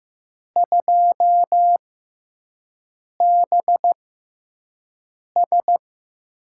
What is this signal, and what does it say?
Morse code, heard as a keyed tone, about 15 wpm, spaced as for 5 wpm, 700 Hz. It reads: 2BS
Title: USA Call Signs - 6 Characters - 6x Character Spacing - 15wpm